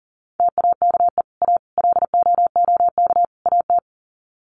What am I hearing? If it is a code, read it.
Morse code, heard as a keyed tone, about 40 wpm, 700 Hz. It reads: TAKE A LOOK AT
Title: Sets of 4 Words 40wpm